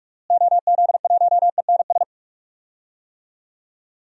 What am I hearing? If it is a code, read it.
Morse code, heard as a keyed tone, about 46 wpm, 700 Hz. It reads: OZ1ENS
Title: DXCC-100 Call Signs - 6 Characters - Repeated 2x - 45wpm